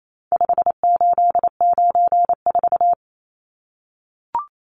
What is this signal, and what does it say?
Morse code, heard as a keyed tone, about 28 wpm, 700 Hz. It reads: 5894
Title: Numbers - 4-Digit 28wpm